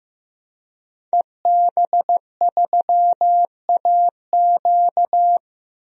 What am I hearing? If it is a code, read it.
Morse code, heard as a keyed tone, about 15 wpm, 700 Hz. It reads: EB3AQ
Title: DXCC-100 Call Signs - 5 Characters - Repeated 2x - 15wpm